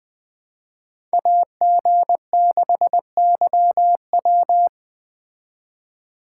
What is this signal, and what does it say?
Morse code, heard as a keyed tone, about 20 wpm, 700 Hz. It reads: AG6YW
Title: Speed Racing - USA 2x2 Call Signs 17wpm